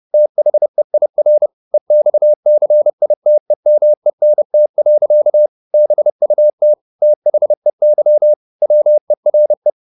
Morse code, heard as a keyed tone, about 30 wpm, 600 Hz. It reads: THEIR EXCITEMENT. BUT THEY WERE